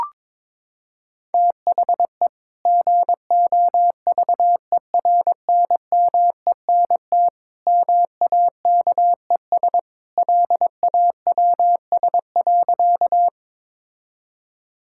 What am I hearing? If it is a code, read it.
Morse code, heard as a keyed tone, about 22 wpm, 700 Hz. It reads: THE GOVERNMENT MAKES LAWS.
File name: practice / 22wpm / Sentences with 2-Word Phrases - 22wpm